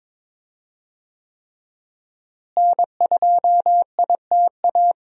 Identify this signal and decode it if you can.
Morse code, heard as a keyed tone, about 22 wpm, 700 Hz. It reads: N2ITA
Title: USA Call Signs - 5 Characters - Repeated 3x - 22wpm